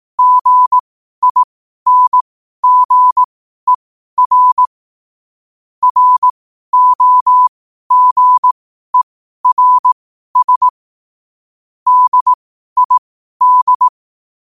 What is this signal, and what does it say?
Morse code, heard as a keyed tone, about 18 wpm, spaced as for 12 wpm, 1000 Hz. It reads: GINGER ROGERS DID
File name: qotdfiles / 12wpm